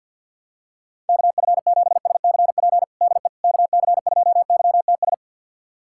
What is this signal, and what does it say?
Morse code, heard as a keyed tone, about 48 wpm, 700 Hz. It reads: KF7SCP DE KC2YTS
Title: USA Callsign DE Callsign - 6 Characters - 50wpm